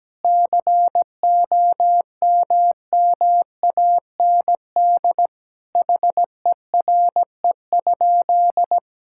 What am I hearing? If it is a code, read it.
Morse code, heard as a keyed tone, about 17 wpm, 700 Hz. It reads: COMMAND HERE?
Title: Sentences from Top 1600 Words 17wpm